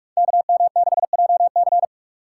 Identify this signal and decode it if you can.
Morse code, heard as a keyed tone, about 44 wpm, 700 Hz. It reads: KM6JC
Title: Code Groups - 5 Letter:Numbers - 45wpm